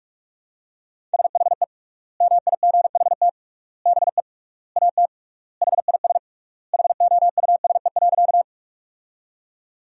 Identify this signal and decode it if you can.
Morse code, heard as a keyed tone, about 41 wpm, 700 Hz. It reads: SHE MIGHT BE AT HIS HOUSE.
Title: 3x Word-Spacing - Sentences from Top 300 Words - 45wpm